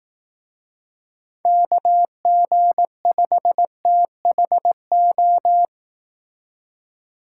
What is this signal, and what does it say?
Morse code, heard as a keyed tone, about 18 wpm, 700 Hz. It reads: KG5THO
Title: Speed Racing - USA 2x3 Call Signs 15wpm